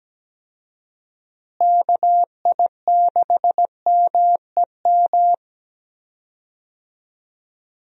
Morse code, heard as a keyed tone, about 17 wpm, 700 Hz. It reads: KI6MEM
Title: USA Call Signs - 6 Characters - Repeated 3x - 17wpm